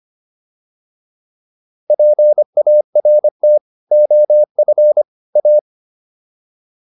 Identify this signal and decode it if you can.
Morse code, heard as a keyed tone, about 25 wpm, 600 Hz. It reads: PART OF A